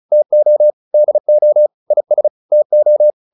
Morse code, heard as a keyed tone, about 34 wpm, 600 Hz. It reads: TO DO IS TO